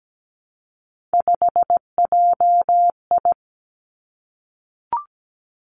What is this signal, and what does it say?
Morse code, heard as a keyed tone, about 17 wpm, 700 Hz. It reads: 5JI
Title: Three Challenging Letter-Number Combinations 17wpm